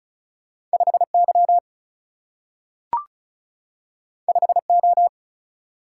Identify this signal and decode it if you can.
Morse code, heard as a keyed tone, about 36 wpm, 700 Hz. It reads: 5 Y 5 O
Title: Single-Letter Single-Number 35wpm